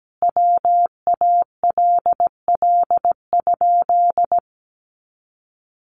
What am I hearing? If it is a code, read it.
Morse code, heard as a keyed tone, about 17 wpm, 700 Hz. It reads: WALL?
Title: Sentences from Top 1100 Words 17wpm